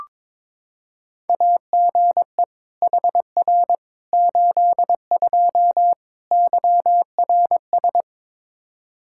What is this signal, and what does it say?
Morse code, heard as a keyed tone, about 22 wpm, 700 Hz. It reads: AGE HR 82 YRS
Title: K1USN SST QSO Practice 22wpm